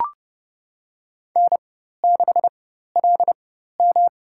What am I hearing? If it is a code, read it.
Morse code, heard as a keyed tone, about 32 wpm, spaced as for 14 wpm, 700 Hz. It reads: N6LM
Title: USA Call Signs - 4 Characters - 4x Character Spacing - 30wpm